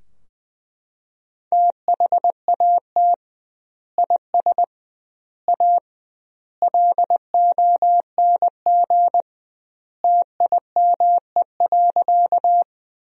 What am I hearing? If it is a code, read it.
Morse code, heard as a keyed tone, about 20 wpm, 700 Hz. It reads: THAT IS A LONG TIME.
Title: 2x Word-Spacing - Sentences from Top 100 Words - 20wpm